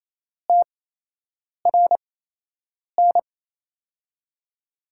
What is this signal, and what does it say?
Morse code, heard as a keyed tone, about 27 wpm, spaced as for 8 wpm, 700 Hz. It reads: TRN
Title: USA Call Signs - 8x Character Spacing - 28wpm